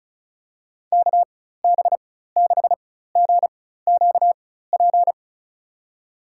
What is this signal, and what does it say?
Morse code, heard as a keyed tone, about 34 wpm, spaced as for 16 wpm, 700 Hz. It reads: KB6GQP
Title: USA Call Signs - 6 Characters - 4x Character Spacing - 35wpm